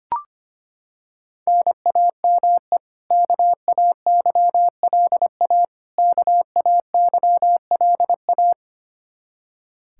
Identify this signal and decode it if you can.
Morse code, heard as a keyed tone, about 25 wpm, 700 Hz. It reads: NAME KAYLA KAYLA
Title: Practice QSO - Exchange 1 - Beginning - 25wpm